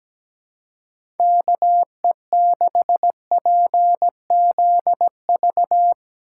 Morse code, heard as a keyed tone, about 17 wpm, 700 Hz. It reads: KE6PZV
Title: USA Call Signs - 6 Characters - 2x Character Spacing - 17wpm